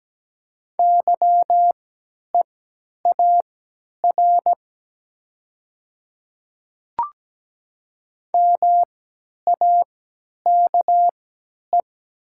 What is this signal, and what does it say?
Morse code, heard as a keyed tone, about 17 wpm, spaced as for 10 wpm, 700 Hz. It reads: YEAR MAKE
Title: 3x Character Spacing - Top 500 Words - 17wpm